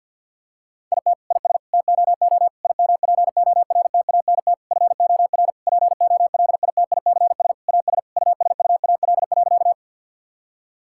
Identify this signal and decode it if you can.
Morse code, heard as a keyed tone, about 50 wpm, 700 Hz. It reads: IT IS TOO IMPORTANT FOR POLITICS AS USUAL.
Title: Sentences from Top 1000 Words - Review Part 6 - 50wpm